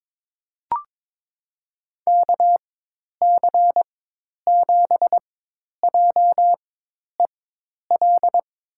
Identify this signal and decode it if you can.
Morse code, heard as a keyed tone, about 22 wpm, spaced as for 10 wpm, 700 Hz. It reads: KC7JEL
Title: USA Call Signs - 4x Character Spacing - 22wpm